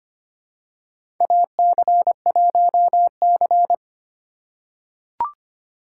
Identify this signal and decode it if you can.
Morse code, heard as a keyed tone, about 25 wpm, 700 Hz. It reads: AC1C
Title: USA Call Signs - 4 Characters - 3x Character Spacing - 25wpm